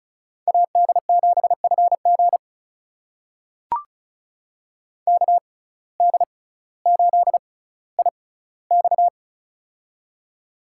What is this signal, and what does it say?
Morse code, heard as a keyed tone, about 34 wpm, spaced as for 18 wpm, 700 Hz. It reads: AD7FG KD8IX
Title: USA Call Signs - 5 Characters - 6x Character Spacing - 35wpm